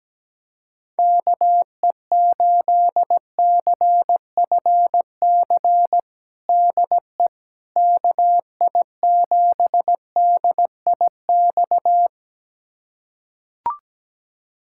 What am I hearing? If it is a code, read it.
Morse code, heard as a keyed tone, about 17 wpm, 700 Hz. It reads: KE8CFC DE KI7DIX
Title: USA Callsign DE Callsign - 6 Characters - 17wpm